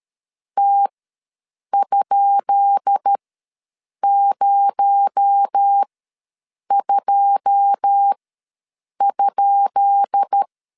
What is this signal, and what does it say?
Morse code, heard as a keyed tone, about 13 wpm, spaced as for 7 wpm, 800 Hz. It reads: T?02?
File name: LEZIONI / 10 SIGNS / 10B INT ws 12wpm 3527